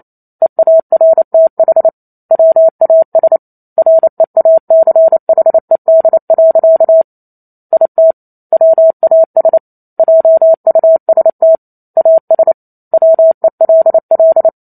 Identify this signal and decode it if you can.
Morse code, heard as a keyed tone, about 28 wpm, 650 Hz. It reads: EARTH WAS REACHED. IT WAS JUST AS WELL